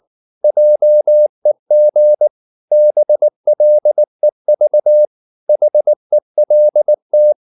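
Morse code, heard as a keyed tone, about 19 wpm, 600 Hz. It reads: JEG BLEV HELT
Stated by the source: Slow (QRS) Morse practice at 18 wpm, CW med Gnister.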